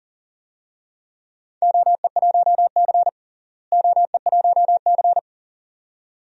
Morse code, heard as a keyed tone, about 40 wpm, 700 Hz. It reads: OE1C OE1C
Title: DXCC-100 Call Signs - 4 Characters - Repeated 2x - 40wpm